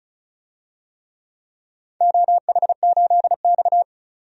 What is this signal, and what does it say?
Morse code, heard as a keyed tone, about 36 wpm, 700 Hz. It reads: OH8X
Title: DXCC-100 Call Signs - 4 Characters - 35wpm